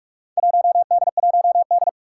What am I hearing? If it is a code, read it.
Morse code, heard as a keyed tone, about 46 wpm, 700 Hz. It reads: 1D1D
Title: Four Challenging Letter-Number Combinations 45wpm